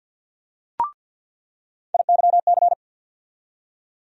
Morse code, heard as a keyed tone, about 47 wpm, 700 Hz. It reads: IYC
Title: USA Call Sign Postfixes - 3 Letters 50wpm